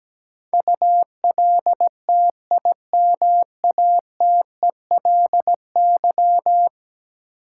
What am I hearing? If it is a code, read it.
Morse code, heard as a keyed tone, about 17 wpm, 700 Hz. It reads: ULTIMATELY
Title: Sentences from Top 1200 Words - New Word Drill - 17wpm